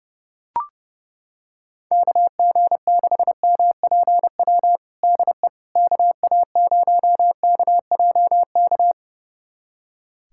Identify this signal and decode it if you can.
Morse code, heard as a keyed tone, about 32 wpm, 700 Hz. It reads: KG6MPW DE KA0KJK
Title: USA Callsign DE Callsign - 6 Characters - 30wpm